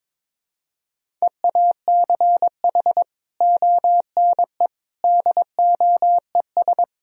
Morse code, heard as a keyed tone, about 21 wpm, 700 Hz. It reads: EACH ONE DOES
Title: Sentences from Top 1000 Words - Review Part 1 - 22wpm